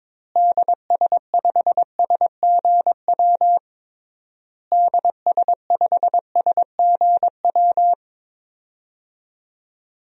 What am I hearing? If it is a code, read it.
Morse code, heard as a keyed tone, about 22 wpm, 700 Hz. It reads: DS5SGW DS5SGW
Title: DXCC-100 Call Signs - 6 Characters - Repeated 3x - 22wpm